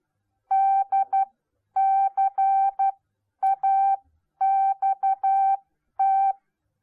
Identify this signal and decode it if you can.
Morse code, heard as a keyed tone, about 12 wpm, spaced as for 10 wpm, 780 Hz. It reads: DCAXT